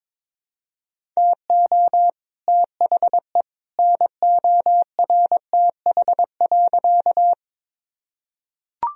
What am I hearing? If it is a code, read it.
Morse code, heard as a keyed tone, about 22 wpm, 700 Hz. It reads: TO THE NORTH.
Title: Sentences from Top 1000 Words - Review Part 1 - 22wpm